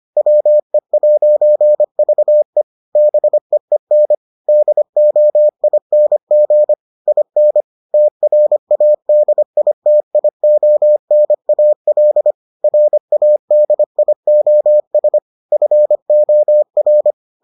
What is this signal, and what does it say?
Morse code, heard as a keyed tone, about 25 wpm, 600 Hz. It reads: WE'VE BEEN DOING IN TRADITIONAL RADIOS FOR